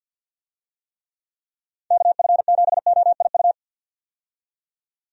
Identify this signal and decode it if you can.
Morse code, heard as a keyed tone, about 47 wpm, 700 Hz. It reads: KF7OIU
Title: USA Call Signs - 6 Characters - 50wpm